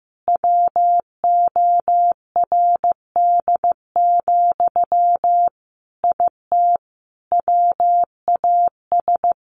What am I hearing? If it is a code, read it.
Morse code, heard as a keyed tone, about 15 wpm, 700 Hz. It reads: WORD, IT WAS
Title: Sentences from Top 1100 Words 15wpm